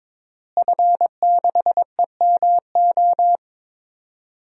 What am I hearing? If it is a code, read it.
Morse code, heard as a keyed tone, about 22 wpm, 700 Hz. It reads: F6EMO
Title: DXCC-100 Call Signs - 5 Characters - Repeated 3x - 22wpm